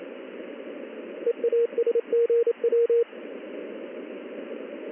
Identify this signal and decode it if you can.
Morse code, heard as a keyed tone, about 28 wpm, 470 Hz. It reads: EASGW
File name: contests / ss-ssb-k5tr- -1700